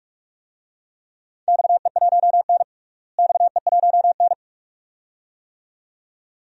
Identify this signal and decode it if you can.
Morse code, heard as a keyed tone, about 46 wpm, 700 Hz. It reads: XE1N XE1N
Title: DXCC-100 Call Signs - 4 Characters - Repeated 2x - 45wpm